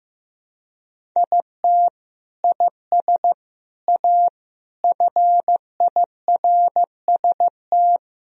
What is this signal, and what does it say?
Morse code, heard as a keyed tone, about 15 wpm, 700 Hz. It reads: IT IS A FIRST
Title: Sentences from Top 1000 Words - Review Part 2 - 15wpm